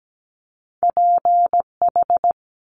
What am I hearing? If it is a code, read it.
Morse code, heard as a keyed tone, about 17 wpm, 700 Hz. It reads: PH
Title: Multi-letter Phonemes Quick Fire 17wpm